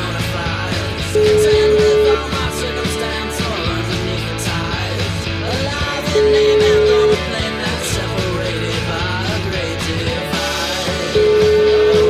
Ringback tone (Europe, 425 Hz), with noise.